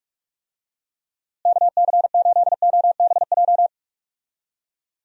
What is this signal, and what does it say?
Morse code, heard as a keyed tone, about 46 wpm, 700 Hz. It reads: KC8OBJ
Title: USA Call Signs - 6 Characters - Repeated 2x - 45wpm